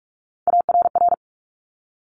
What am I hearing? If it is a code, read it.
Morse code, heard as a keyed tone, about 46 wpm, 700 Hz. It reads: ARR
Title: Top Three Letter Combinations in Words 45wpm